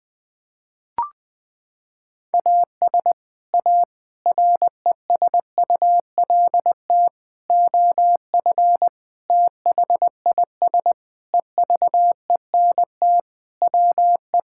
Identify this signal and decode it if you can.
Morse code, heard as a keyed tone, about 20 wpm, 700 Hz. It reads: AS A RESULT OF THIS EVENT WE
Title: Sentences with 5-Word Phrases - 20wpm